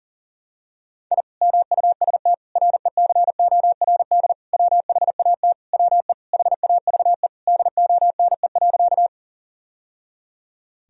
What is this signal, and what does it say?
Morse code, heard as a keyed tone, about 40 wpm, 700 Hz. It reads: I MUST RECORD WHAT WE HAVE DONE.
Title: Sentences from Top 1000 Words - Review Part 3 - 40wpm